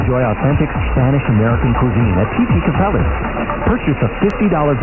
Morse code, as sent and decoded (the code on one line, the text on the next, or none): none